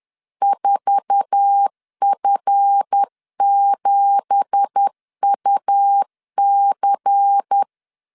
Morse code, heard as a keyed tone, about 10 wpm, 800 Hz. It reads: 4F7UC
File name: CW / DET / 1 9 1500C S B 10wpm 4439